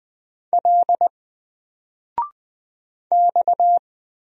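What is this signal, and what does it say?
Morse code, heard as a keyed tone, about 20 wpm, 700 Hz. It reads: L X
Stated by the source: Single letter and number Morse drill at 20 wpm, from N4LCD.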